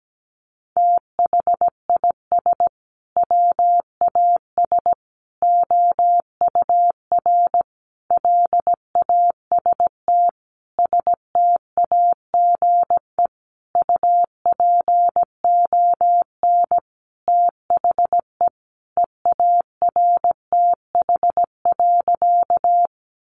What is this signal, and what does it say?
Morse code, heard as a keyed tone, about 17 wpm, 700 Hz. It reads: THIS WAS OUR LAST STAGE UPON THE EARTH.
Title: Sentences from Top 1200 Words 17wpm